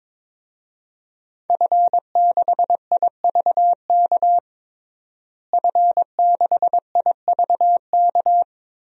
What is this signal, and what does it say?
Morse code, heard as a keyed tone, about 22 wpm, 700 Hz. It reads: F6IVK F6IVK
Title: DXCC-100 Call Signs - 5 Characters - Repeated 3x - 22wpm